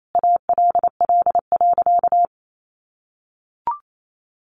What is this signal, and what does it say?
Morse code, heard as a keyed tone, about 28 wpm, 700 Hz. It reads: ALL.
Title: Sentences from Top 1300 Words 28wpm